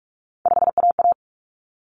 Morse code, heard as a keyed tone, about 44 wpm, 700 Hz. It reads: 5AA
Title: Common Cut Numbers - 3-Digit 45wpm